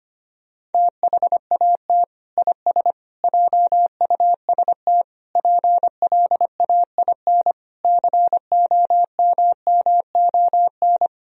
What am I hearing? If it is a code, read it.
Morse code, heard as a keyed tone, about 25 wpm, 700 Hz. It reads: THAT IS JUST PLAIN COMMON
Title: Sentences from Top 1000 Words - Review Part 4 - 25wpm